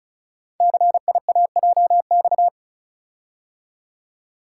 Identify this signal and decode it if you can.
Morse code, heard as a keyed tone, about 36 wpm, 690 Hz. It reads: CIAJX
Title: Code Groups - 5 Letters - 35wpm